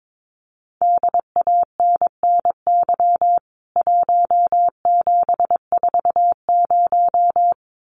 Morse code, heard as a keyed tone, about 22 wpm, 700 Hz. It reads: DANNY 1740
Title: CWT Exchange 22wpm